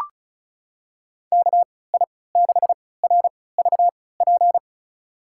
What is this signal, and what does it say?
Morse code, heard as a keyed tone, about 34 wpm, spaced as for 20 wpm, 700 Hz. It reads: KI6RVP
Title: USA Call Signs - 6 Characters - 3x Character Spacing - 35wpm